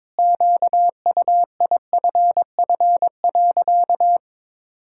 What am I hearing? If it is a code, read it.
Morse code, heard as a keyed tone, about 22 wpm, 700 Hz. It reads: QUIFF.
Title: Pangram Challenge v2 - 22wpm